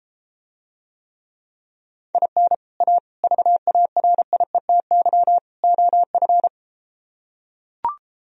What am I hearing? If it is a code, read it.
Morse code, heard as a keyed tone, about 33 wpm, 700 Hz. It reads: IN A VARIETY OF